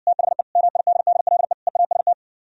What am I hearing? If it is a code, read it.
Morse code, heard as a keyed tone, about 59 wpm, 700 Hz. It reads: THE MIDDLE EAST